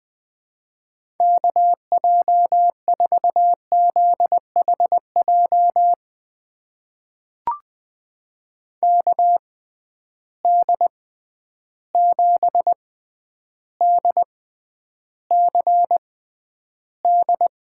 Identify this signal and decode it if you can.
Morse code, heard as a keyed tone, about 20 wpm, spaced as for 10 wpm, 700 Hz. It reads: KJ4ZHJ KD7DCD
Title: USA Call Signs - 6 Characters - 6x Character Spacing - 20wpm